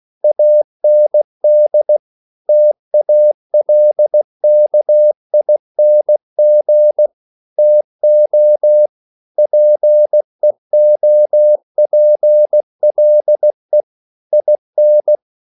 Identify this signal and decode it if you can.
Morse code, heard as a keyed tone, about 16 wpm, 600 Hz. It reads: AND TALKING TO PEOPLE IN